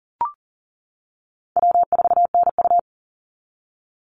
Morse code, heard as a keyed tone, about 40 wpm, 700 Hz. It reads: W4NU
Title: Easy Call Signs - 40wpm